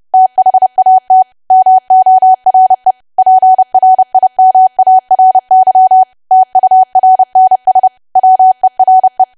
Morse code, heard as a keyed tone, about 30 wpm, 750 Hz. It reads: THAT MORE PRIMARY TURNS WERE
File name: cwa / W1AW-Level3 / 30WPM